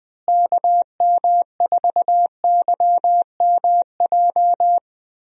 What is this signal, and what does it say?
Morse code, heard as a keyed tone, about 20 wpm, 700 Hz. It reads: KM4YMJ
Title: USA Call Signs - 6 Characters - Repeated 2x - 20wpm